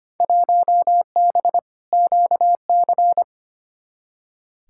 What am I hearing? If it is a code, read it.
Morse code, heard as a keyed tone, about 25 wpm, 700 Hz. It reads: 1B QC
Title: Field Day Exchanges Encore - 25wpm